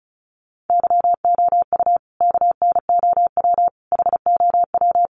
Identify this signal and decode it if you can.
Morse code, heard as a keyed tone, about 34 wpm, 700 Hz. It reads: YOU KNOW HOW
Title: Sentences from Top 1100 Words 35wpm